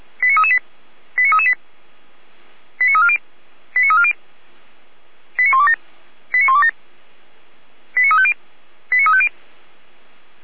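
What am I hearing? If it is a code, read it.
Five-tone selective call (ZVEI), heard as quick runs of five tones, tones of about 80 ms. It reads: 89338 89338 89340 89340 89127 89127 89350 89350
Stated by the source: ZVEI five-tone call 89320 received on SDR.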